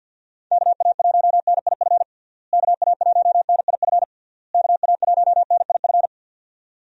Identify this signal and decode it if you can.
Morse code, heard as a keyed tone, about 48 wpm, 700 Hz. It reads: KA1NIF KA1NIF KA1NIF
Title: USA Call Signs - 6 Characters - Repeated 3x - 50wpm